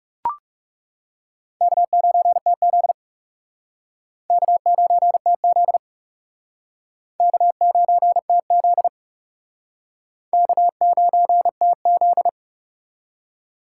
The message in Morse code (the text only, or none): K9TZ K9TZ K9TZ K9TZ